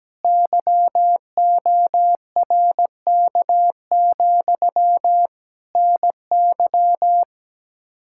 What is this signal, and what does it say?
Morse code, heard as a keyed tone, about 17 wpm, 700 Hz. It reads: YORK, NY
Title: K1USN SST QSO Practice 17wpm